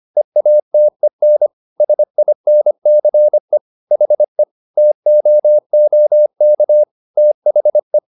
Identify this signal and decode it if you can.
Morse code, heard as a keyed tone, about 25 wpm, 600 Hz. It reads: EATEN SINCE HE TOOK THE